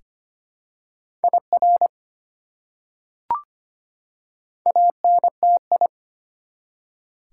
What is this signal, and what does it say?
Morse code, heard as a keyed tone, about 25 wpm, 700 Hz. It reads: IR ANTI
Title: Prefixes and Suffixes 25wpm